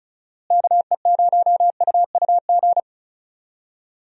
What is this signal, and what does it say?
Morse code, heard as a keyed tone, about 36 wpm, 700 Hz. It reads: KE0UUG
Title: USA Call Signs - 6 Characters - Repeated 2x - 35wpm